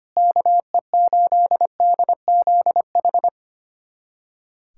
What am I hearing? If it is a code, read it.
Morse code, heard as a keyed tone, about 25 wpm, 700 Hz. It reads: KE8DZH
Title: USA Callsign DE Callsign 25wpm